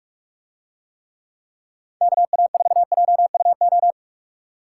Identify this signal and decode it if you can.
Morse code, heard as a keyed tone, about 42 wpm, 700 Hz. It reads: KA4JUO